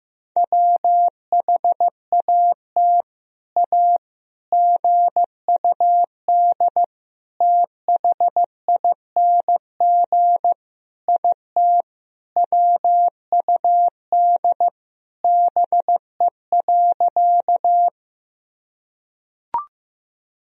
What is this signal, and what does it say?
Morse code, heard as a keyed tone, about 15 wpm, 700 Hz. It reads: WHAT A GUD THING IT WUD BE.
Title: Sentences from Top 200 Words and Abbreviations 15wpm